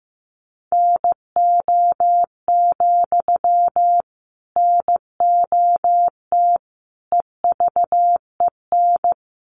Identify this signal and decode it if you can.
Morse code, heard as a keyed tone, about 15 wpm, 700 Hz. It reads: NO, NOT EVEN
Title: Sentences from Top 1300 Words 15wpm